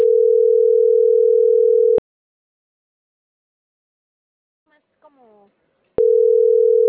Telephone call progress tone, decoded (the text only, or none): ringback tone